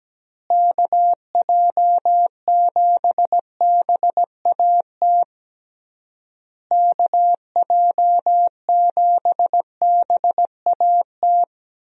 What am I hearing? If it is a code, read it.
Morse code, heard as a keyed tone, about 17 wpm, 700 Hz. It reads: KJ7BAT KJ7BAT
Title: USA Call Signs - 6 Characters - Repeated 3x - 17wpm